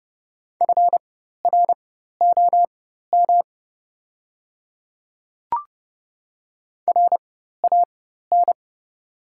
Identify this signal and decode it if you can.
Morse code, heard as a keyed tone, about 30 wpm, spaced as for 14 wpm, 690 Hz. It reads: FROM RAN
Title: 4x Character Spacing - Top 500 Words - 30wpm